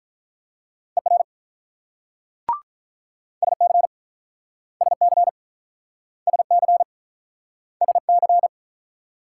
Morse code, heard as a keyed tone, about 46 wpm, 700 Hz. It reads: ER SC SC SC SC